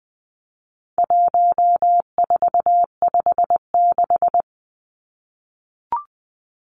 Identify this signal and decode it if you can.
Morse code, heard as a keyed tone, about 20 wpm, 700 Hz. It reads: 1456
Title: Numbers - 4-Digit 20wpm